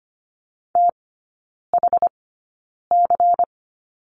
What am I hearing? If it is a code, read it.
Morse code, heard as a keyed tone, about 25 wpm, spaced as for 9 wpm, 700 Hz. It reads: THC
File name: course / 25:10wpm / 093 Lesson 16e - Letter c - Three Characters 25wpm